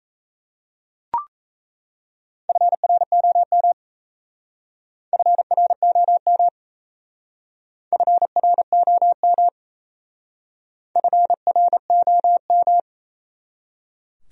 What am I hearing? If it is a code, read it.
Morse code, heard as a keyed tone, about 34 wpm, 700 Hz. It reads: FROM FROM FROM FROM
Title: Speed-Racing Top 100 Words 28wpm